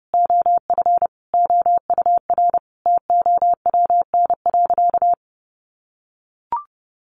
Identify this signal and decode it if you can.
Morse code, heard as a keyed tone, about 30 wpm, 700 Hz. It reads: OF OUR TOWN.
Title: Sentences from Top 1500 Words 30wpm